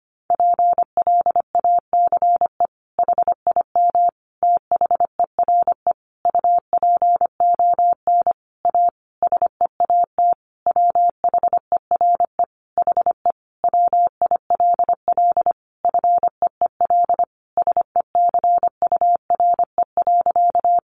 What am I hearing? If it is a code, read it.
Morse code, heard as a keyed tone, about 25 wpm, 700 Hz. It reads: PLACE HIM THERE UPON A SEAT WHERE HE WILL FEEL SECURE.
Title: Sentences from Top 1500 Words 25wpm